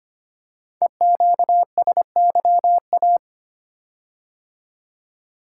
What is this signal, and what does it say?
Morse code, heard as a keyed tone, about 25 wpm, 700 Hz. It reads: EQSYA
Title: Code Groups - 5 Letters - 25wpm